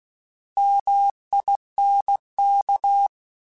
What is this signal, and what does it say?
Morse code, heard as a keyed tone, about 16 wpm, 800 Hz. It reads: MINK
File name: Alphabetical / Alphabetical mp3s / Ms @15